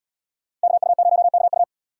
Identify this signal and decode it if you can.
Morse code, heard as a keyed tone, about 66 wpm, 700 Hz. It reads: KU0KU